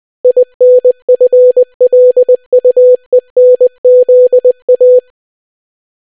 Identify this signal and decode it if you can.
Morse code, heard as a keyed tone, about 20 wpm, 500 Hz. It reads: INFLUENZA